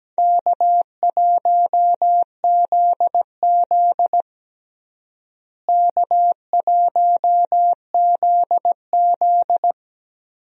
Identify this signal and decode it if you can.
Morse code, heard as a keyed tone, about 17 wpm, 700 Hz. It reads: K1ZZ K1ZZ
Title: USA Call Signs - 4 Characters - Repeated 2x - 17wpm